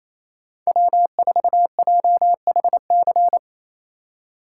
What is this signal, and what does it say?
Morse code, heard as a keyed tone, about 28 wpm, 700 Hz. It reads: W4JHC
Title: USA Call Signs - 5 Characters - 2x Character Spacing - 28wpm